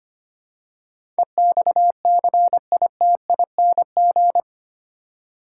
Morse code, heard as a keyed tone, about 25 wpm, 700 Hz. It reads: EXCITING
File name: 8 Letter Words - First 3 Letters Given - 25wpm